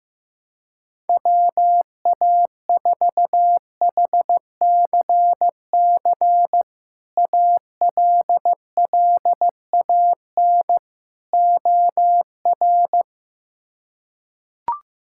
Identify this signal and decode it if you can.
Morse code, heard as a keyed tone, about 15 wpm, 700 Hz. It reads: WA4HCC ALLAN OR
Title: K1USN SST QSO Practice - 15wpm